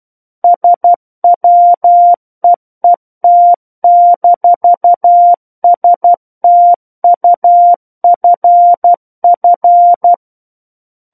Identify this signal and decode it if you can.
Morse code, heard as a keyed tone, about 12 wpm, 700 Hz. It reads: SWEET-STUFF
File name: Book One / 03-Chapter 3 - 12 WPM